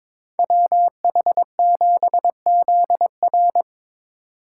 Morse code, heard as a keyed tone, about 25 wpm, 700 Hz. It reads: WH7ZR
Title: Speed Racing - USA 2x2 Call Signs 22wpm